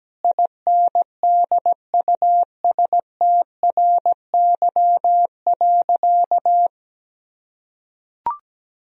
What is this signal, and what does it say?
Morse code, heard as a keyed tone, about 17 wpm, 700 Hz. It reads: INDUSTRY.
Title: Sentences from Top 1000 Words - Review Part 4 - 17wpm